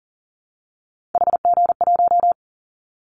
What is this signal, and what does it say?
Morse code, heard as a keyed tone, about 42 wpm, 700 Hz. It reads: HG1
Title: Callsign Prefixes for Top 50 DXCC - 40wpm